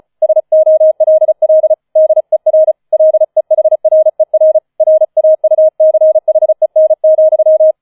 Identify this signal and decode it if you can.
Morse code, heard as a keyed tone, about 36 wpm, 630 Hz. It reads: SOLL DER LEHRER RAUCHEN,
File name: 004 Vierter Streich 35wpm